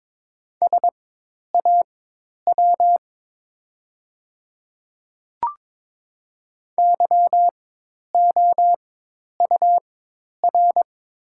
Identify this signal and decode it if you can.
Morse code, heard as a keyed tone, about 22 wpm, spaced as for 10 wpm, 690 Hz. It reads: SAW YOUR
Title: 4x Character Spacing - Top 300 Words - 22wpm